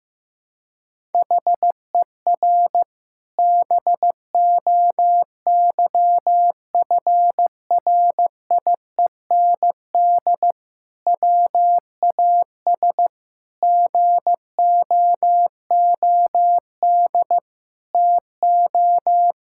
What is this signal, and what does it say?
Morse code, heard as a keyed tone, about 15 wpm, 700 Hz. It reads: HER BOYFRIEND WAS GOOD TO
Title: Sentences from Top 1000 Words - Review Part 5 - 15wpm